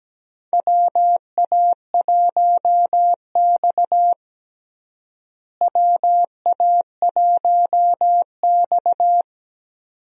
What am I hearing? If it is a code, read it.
Morse code, heard as a keyed tone, about 17 wpm, 700 Hz. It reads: WA1X WA1X
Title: USA Call Signs - 4 Characters - Repeated 2x - 17wpm